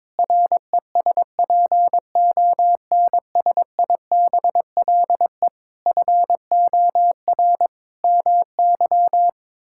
Morse code, heard as a keyed tone, about 22 wpm, 700 Hz. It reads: RESPONSIBLE FOR MY
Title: Sentences from Top 1600 Words 22wpm